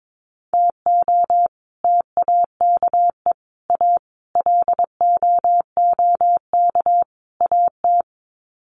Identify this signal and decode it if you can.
Morse code, heard as a keyed tone, about 22 wpm, 700 Hz. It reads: TO TAKE A LOOK AT
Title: Sets of 5 Words 22wpm